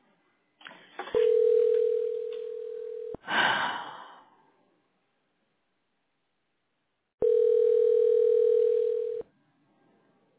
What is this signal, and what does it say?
Ringback tone (North America), 440 + 480 Hz.